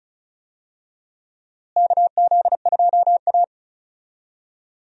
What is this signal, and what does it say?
Morse code, heard as a keyed tone, about 34 wpm, 700 Hz. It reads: KZ2A
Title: USA Call Signs - 4 Characters - Repeated 3x - 35wpm